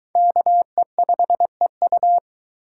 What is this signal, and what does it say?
Morse code, heard as a keyed tone, about 23 wpm, 700 Hz. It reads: KE5EU